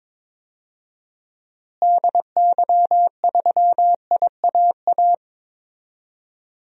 Morse code, heard as a keyed tone, about 22 wpm, 700 Hz. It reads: DY3IAA
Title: DXCC-100 Call Signs - 6 Characters - Repeated 3x - 22wpm